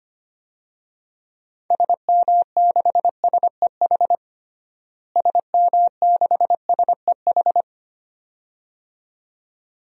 Morse code, heard as a keyed tone, about 25 wpm, 700 Hz. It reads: SM6SEH SM6SEH
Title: DXCC-100 Call Signs - 6 Characters - Repeated 2x - 25wpm